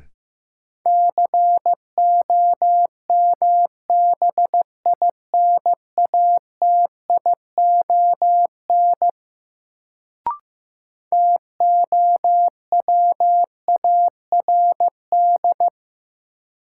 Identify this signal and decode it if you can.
Morse code, heard as a keyed tone, about 15 wpm, 700 Hz. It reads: COMBINATION TOWARD
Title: Sentences from Top 1000 Words - Review Part 6 - New Word Drill - 15wpm